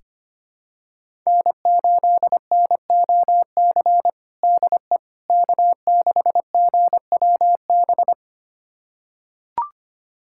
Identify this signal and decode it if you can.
Morse code, heard as a keyed tone, about 25 wpm, 700 Hz. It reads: N8NOC DE K6GWB
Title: USA Callsign DE Callsign - 5 Characters - 25wpm